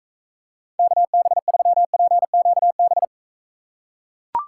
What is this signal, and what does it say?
Morse code, heard as a keyed tone, about 42 wpm, 700 Hz. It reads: KB3PQB